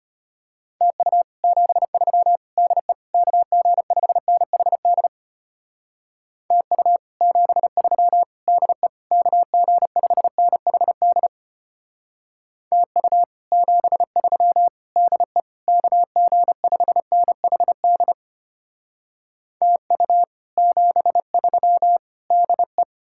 Morse code, heard as a keyed tone, about 32 wpm, 700 Hz. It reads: TU 73 DE KG5NHD TU 73 DE KG5NHD TU 73 DE KG5NHD TU 73 DE